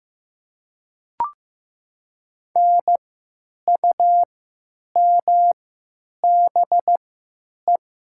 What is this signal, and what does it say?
Morse code, heard as a keyed tone, about 15 wpm, spaced as for 9 wpm, 700 Hz. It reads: NUMBE
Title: 3x Character Spacing - Top 200 Words - 15wpm